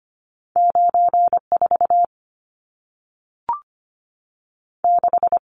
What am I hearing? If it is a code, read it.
Morse code, heard as a keyed tone, about 25 wpm, 700 Hz. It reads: 94 6
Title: Numbers - 2-Digit 25wpm